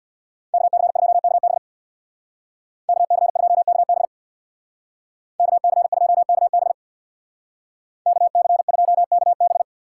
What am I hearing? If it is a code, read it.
Morse code, heard as a keyed tone, about 60 wpm, 700 Hz. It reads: KC2KB KC2KB KC2KB KC2KB
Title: Speed Racing - USA Call Signs 50wpm